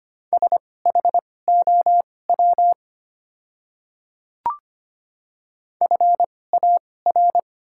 Morse code, heard as a keyed tone, about 25 wpm, spaced as for 18 wpm, 700 Hz. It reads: SHOW FAR
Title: 2x Character Spacing - Top 300 Words - 25wpm